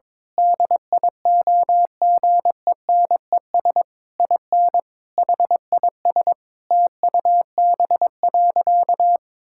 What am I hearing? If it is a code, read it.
Morse code, heard as a keyed tone, about 22 wpm, 700 Hz. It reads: DIOGENES IN HIS TUB.
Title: Pangram Challenge v2 - 22wpm